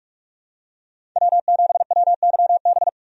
Morse code, heard as a keyed tone, about 47 wpm, 700 Hz. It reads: W7WYB
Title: USA Call Signs - 5 Characters - 45wpm